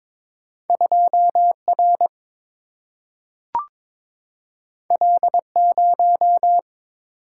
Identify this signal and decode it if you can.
Morse code, heard as a keyed tone, about 22 wpm, 700 Hz. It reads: 2R L0
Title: Single-Letter Single-Number 22wpm